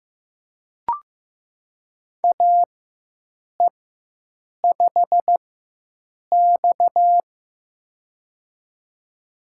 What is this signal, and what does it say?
Morse code, heard as a keyed tone, about 15 wpm, spaced as for 7 wpm, 700 Hz. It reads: AE5X
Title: USA Call Signs - 4x Character Spacing - 15wpm